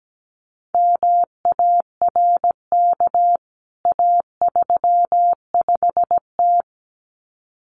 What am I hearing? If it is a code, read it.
Morse code, heard as a keyed tone, about 17 wpm, 700 Hz. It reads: MARK A35T
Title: CWT Exchange 17wpm